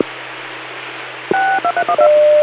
DTMF, heard as telephone touch tones, tones of about 80 ms. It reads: 6231